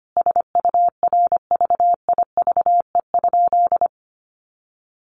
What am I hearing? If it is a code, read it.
Morse code, heard as a keyed tone, about 25 wpm, 700 Hz. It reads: SURVIVE?
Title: A Sound Of Thunder 25wpm